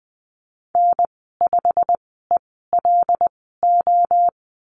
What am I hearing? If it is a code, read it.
Morse code, heard as a keyed tone, about 20 wpm, spaced as for 14 wpm, 700 Hz. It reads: N5ELO
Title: 122 Lesson 20g - Number 3 - Call Signs 20wpm